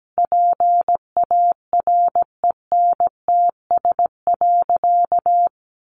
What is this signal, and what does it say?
Morse code, heard as a keyed tone, about 17 wpm, 700 Hz. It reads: PARENTS.
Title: Easy Phrases 17wpm